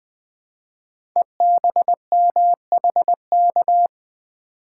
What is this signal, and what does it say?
Morse code, heard as a keyed tone, about 20 wpm, 700 Hz. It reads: EBMHK
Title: Code Groups - 5 Letters - 20wpm